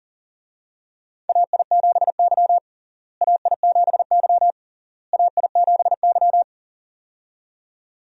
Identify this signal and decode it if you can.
Morse code, heard as a keyed tone, about 40 wpm, 700 Hz. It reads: AI7Y AI7Y AI7Y
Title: USA Call Signs - 4 Characters - Repeated 3x - 40wpm